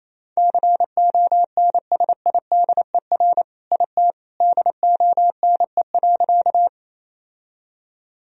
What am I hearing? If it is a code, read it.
Morse code, heard as a keyed tone, about 27 wpm, 700 Hz. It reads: CONSIDER IT DONE.
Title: Sentences from Top 1000 Words - Review Part 4 - 28wpm